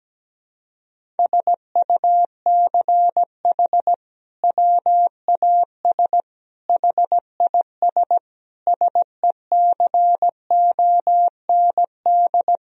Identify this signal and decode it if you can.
Morse code, heard as a keyed tone, about 17 wpm, 700 Hz. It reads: SUCH WAS HIS SECOND
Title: Sentences from Top 1000 Words - Review Part 1 - 17wpm